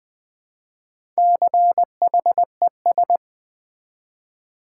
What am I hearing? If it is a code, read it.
Morse code, heard as a keyed tone, about 20 wpm, 700 Hz. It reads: CHES
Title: Build A Word - 1 Letter At A Time - 5-Letter Words - 20wpm